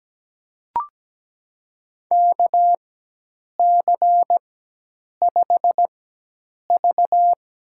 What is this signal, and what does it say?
Morse code, heard as a keyed tone, about 17 wpm, spaced as for 8 wpm, 700 Hz. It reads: KC5V
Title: USA Call Signs - 4x Character Spacing - 17wpm